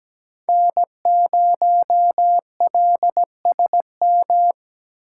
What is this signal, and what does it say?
Morse code, heard as a keyed tone, about 17 wpm, 700 Hz. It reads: N0LSM
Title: USA Call Signs - 5 Characters - 4x Character Spacing - 17wpm